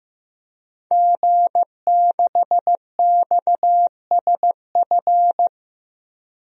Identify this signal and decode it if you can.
Morse code, heard as a keyed tone, about 15 wpm, 700 Hz. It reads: G6XSF
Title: DXCC-100 Call Signs - 5 Characters - 15wpm